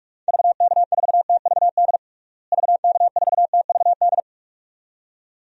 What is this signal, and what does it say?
Morse code, heard as a keyed tone, about 42 wpm, 700 Hz. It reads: VK4TVD VK4TVD
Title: DXCC-100 Call Signs - 6 Characters - Repeated 2x - 45wpm